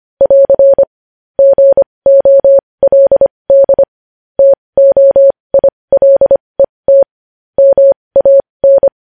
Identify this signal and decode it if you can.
Morse code, heard as a keyed tone, about 25 wpm, spaced as for 20 wpm, 550 Hz. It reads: <AR> GOLD TOILET MAN
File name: podcast / News headlines-1x-25x20